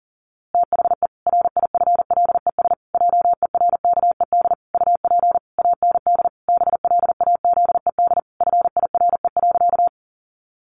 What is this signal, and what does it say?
Morse code, heard as a keyed tone, about 40 wpm, 700 Hz. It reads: THE RIFLES JERKED UP AND BLAZED FIRE.
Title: A Sound Of Thunder 40wpm